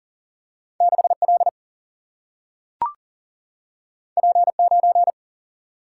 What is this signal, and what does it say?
Morse code, heard as a keyed tone, about 40 wpm, 700 Hz. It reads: 6 L P 9
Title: Single-Letter Single-Number 40wpm